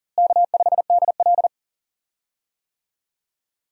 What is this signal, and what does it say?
Morse code, heard as a keyed tone, about 40 wpm, 700 Hz. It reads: K5DL
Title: USA Call Signs - 4 Characters - Repeated 3x - 40wpm